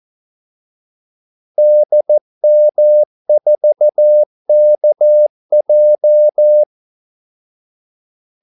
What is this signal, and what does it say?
Morse code, heard as a keyed tone, about 14 wpm, 600 Hz. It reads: DM4KJ
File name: Callsigns-14wpm